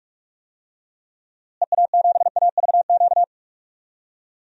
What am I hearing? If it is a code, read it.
Morse code, heard as a keyed tone, about 46 wpm, 700 Hz. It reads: EA7AVQ
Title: DXCC-100 Call Signs - 6 Characters - Repeated 3x - 45wpm